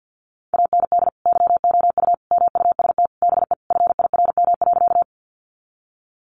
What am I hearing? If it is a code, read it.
Morse code, heard as a keyed tone, about 48 wpm, 700 Hz. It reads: AND YOU MUST BE FIRM.
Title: Sentences from Top 1400 Words 50wpm